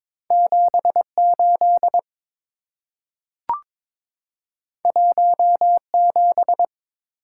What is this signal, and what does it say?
Morse code, heard as a keyed tone, about 22 wpm, 690 Hz. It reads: 78 17
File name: practice / 22wpm / Two Letter-Number Combinations - 22wpm